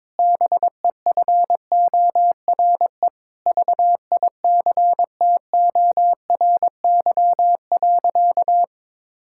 Morse code, heard as a keyed tone, about 21 wpm, 700 Hz. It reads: BEFORE VICTORY.
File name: Sentences with 9-Word Phrases - 22wpm